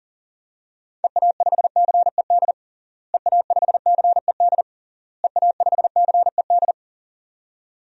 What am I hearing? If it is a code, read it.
Morse code, heard as a keyed tone, about 40 wpm, 700 Hz. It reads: EA5CED EA5CED EA5CED
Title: DXCC-100 Call Signs - 6 Characters - Repeated 3x - 40wpm